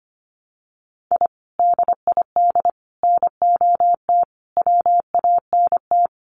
Morse code, heard as a keyed tone, about 25 wpm, 700 Hz. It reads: I DID NOT WANT
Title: Sets of 5 Words 25wpm